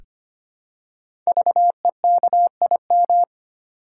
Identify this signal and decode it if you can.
Morse code, heard as a keyed tone, about 25 wpm, 700 Hz. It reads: VEKIM
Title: Code Groups - 5 Letters - 25wpm